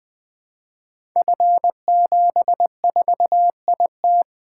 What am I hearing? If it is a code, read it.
Morse code, heard as a keyed tone, about 20 wpm, 700 Hz. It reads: F74IT
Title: Code Groups - 5 Letter:Numbers - 20wpm